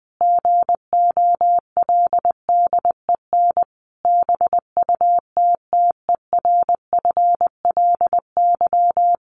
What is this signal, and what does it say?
Morse code, heard as a keyed tone, about 20 wpm, 700 Hz. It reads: GOLDEN BUTTERFLY
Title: A Sound Of Thunder 20wpm